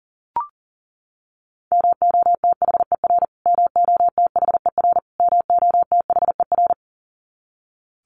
Morse code, heard as a keyed tone, about 40 wpm, 700 Hz. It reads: MOTHER MOTHER MOTHER
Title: Top 300 Words Repeated 40wpm